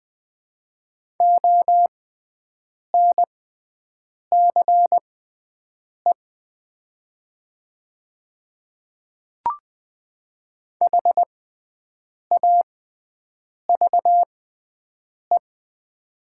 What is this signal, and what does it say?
Morse code, heard as a keyed tone, about 20 wpm, spaced as for 7 wpm, 700 Hz. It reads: ONCE HAVE
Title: 6x Character Spacing - Top 500 Words - 20wpm